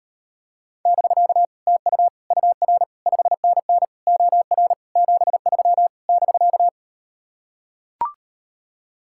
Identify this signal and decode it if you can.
Morse code, heard as a keyed tone, about 39 wpm, 700 Hz. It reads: <BK> TU UR 5NN OR 73 <BK>